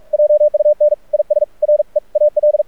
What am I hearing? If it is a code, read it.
Morse code, heard as a keyed tone, about 44 wpm, 600 Hz. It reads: JUN IS REAL
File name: archived / CRSnet